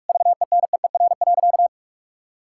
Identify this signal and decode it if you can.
Morse code, heard as a keyed tone, about 41 wpm, 700 Hz. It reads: VENEER.
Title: Pangram Challenge v2 - 45wpm